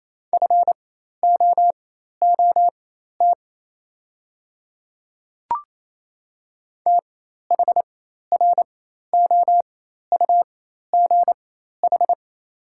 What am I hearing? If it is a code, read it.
Morse code, heard as a keyed tone, about 28 wpm, spaced as for 13 wpm, 700 Hz. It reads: FOOT THROUGH
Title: 4x Character Spacing - Top 500 Words - 28wpm